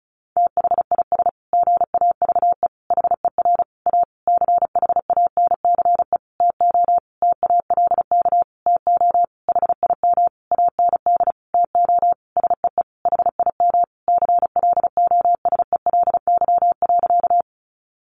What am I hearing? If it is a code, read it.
Morse code, heard as a keyed tone, about 34 wpm, 700 Hz. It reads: THIS GAVE HER A CHANCE TO TALK TO HIM AND TO SEE HIM CLOSELY.
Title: Sentences from Top 1400 Words 35wpm